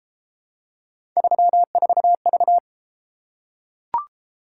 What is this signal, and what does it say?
Morse code, heard as a keyed tone, about 29 wpm, 700 Hz. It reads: 34V